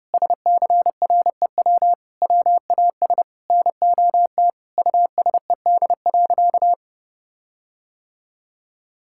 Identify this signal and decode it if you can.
Morse code, heard as a keyed tone, about 30 wpm, 700 Hz. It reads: SCREW WAS NOT USED.
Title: Sentences from Top 1600 Words 30wpm